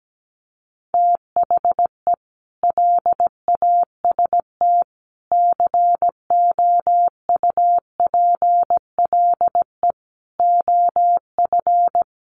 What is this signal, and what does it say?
Morse code, heard as a keyed tone, about 17 wpm, 700 Hz. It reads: THE LAST COUPLE OF
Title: Sets of 5 Words 17wpm